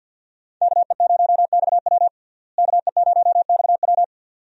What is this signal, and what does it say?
Morse code, heard as a keyed tone, about 53 wpm, 700 Hz. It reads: KE0XW KE0XW
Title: USA Call Signs - 5 Characters - Repeated 2x - 50wpm